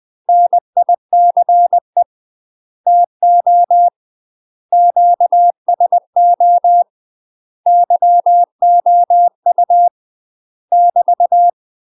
Morse code, heard as a keyed tone, about 20 wpm, 700 Hz. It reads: NICE TO QSO YOU <BT>
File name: UK MASTER QSOs TRAINING / 20WPM / UK MASTER QSOs 0065